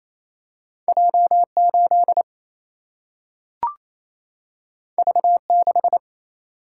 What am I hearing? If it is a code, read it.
Morse code, heard as a keyed tone, about 28 wpm, 700 Hz. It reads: J 8 V 6
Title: Single-Letter Single-Number 28wpm